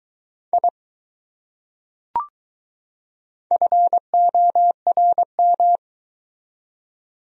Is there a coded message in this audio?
Yes: Morse code from 0.5 to 5.8 s, heard as a keyed tone, about 20 wpm, 700 Hz. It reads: I FORM